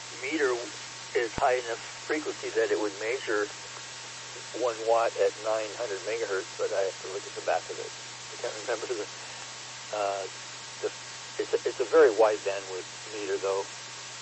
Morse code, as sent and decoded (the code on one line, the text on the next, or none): none